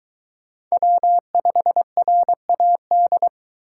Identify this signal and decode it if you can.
Morse code, heard as a keyed tone, about 21 wpm, 700 Hz. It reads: W5RAD